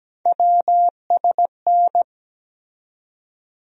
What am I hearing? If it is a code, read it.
Morse code, heard as a keyed tone, about 17 wpm, 700 Hz. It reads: WSN
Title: USA Call Sign Postfixes - 3 Letters 17wpm